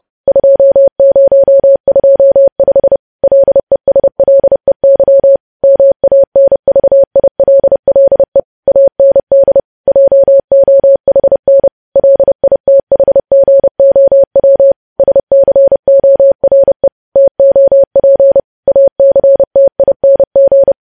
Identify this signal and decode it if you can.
Morse code, heard as a keyed tone, about 30 wpm, 550 Hz. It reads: 2025 LESLEY MANVILLE AND JOHN LITHGOW SCORE TOP ACTING
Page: news headlines in morse code at 30 wpm